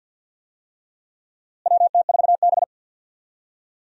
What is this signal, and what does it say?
Morse code, heard as a keyed tone, about 52 wpm, 700 Hz. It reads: WT4B
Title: USA Call Signs - 4 Characters - Repeated 3x - 50wpm